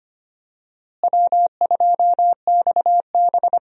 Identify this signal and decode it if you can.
Morse code, heard as a keyed tone, about 25 wpm, 700 Hz. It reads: W2XB
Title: USA Call Signs - 4 Characters - Repeated 3x - 25wpm